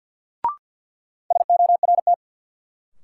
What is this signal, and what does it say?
Morse code, heard as a keyed tone, about 51 wpm, 700 Hz. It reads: SORT
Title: Sentences from Top 1300 Words - New Word Drill Alt - 50wpm